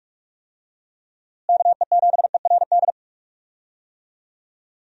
Morse code, heard as a keyed tone, about 47 wpm, 700 Hz. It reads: KE7ERD
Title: USA Call Signs - 6 Characters - Repeated 2x - 45wpm